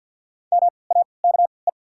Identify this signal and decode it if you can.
Morse code, heard as a keyed tone, about 48 wpm, spaced as for 28 wpm, 700 Hz. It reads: MAKE
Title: 3x Character Spacing - Top 300 Words - 50wpm